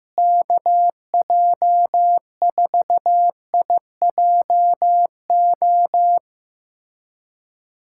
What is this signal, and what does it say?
Morse code, heard as a keyed tone, about 16 wpm, 700 Hz. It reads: KJ4IJO
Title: USA Callsign DE Callsign - 6 Characters - 15wpm